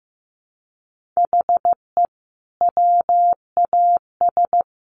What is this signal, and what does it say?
Morse code, heard as a keyed tone, about 15 wpm, 700 Hz. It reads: HE WAS
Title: Sentences from Top 1500 Words 15wpm